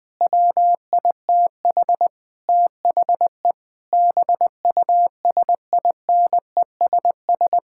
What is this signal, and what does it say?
Morse code, heard as a keyed tone, about 20 wpm, 700 Hz. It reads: WITH THE BUSINESS
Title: Sentences from Top 1000 Words - Review Part 6 - 20wpm